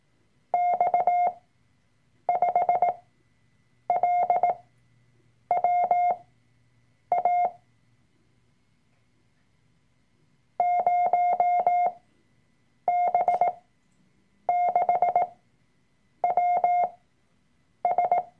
Morse code, heard as a keyed tone, about 18 wpm, spaced as for 7 wpm, 710 Hz. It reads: X5LWA 0B6WS
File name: Workshop Dateien / CW-Online-Kurs / Lektion 003